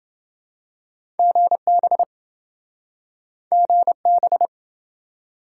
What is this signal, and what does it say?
Morse code, heard as a keyed tone, about 26 wpm, 700 Hz. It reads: GB GB